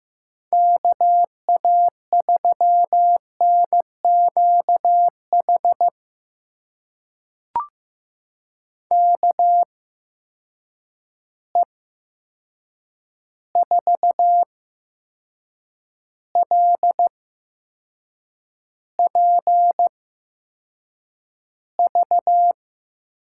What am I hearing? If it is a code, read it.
Morse code, heard as a keyed tone, about 15 wpm, spaced as for 6 wpm, 700 Hz. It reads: KA3NQH KE4LPV
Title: USA Call Signs - 6 Characters - 8x Character Spacing - 15wpm